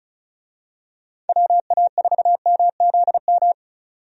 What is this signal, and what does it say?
Morse code, heard as a keyed tone, about 36 wpm, 700 Hz. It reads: WA4MZM
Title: USA Call Signs - 6 Characters - 35wpm